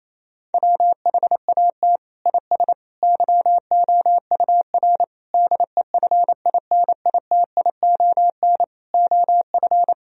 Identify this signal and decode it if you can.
Morse code, heard as a keyed tone, about 28 wpm, 700 Hz. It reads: WHAT IS YOUR DEFINITION OF
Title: Sentences from Top 1000 Words - Review Part 4 - 28wpm